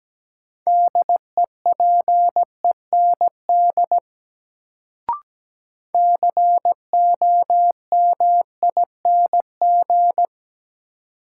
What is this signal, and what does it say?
Morse code, heard as a keyed tone, about 17 wpm, 700 Hz. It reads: DEPEND COMING
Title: Sentences from Top 1300 Words - New Word Drill - 17wpm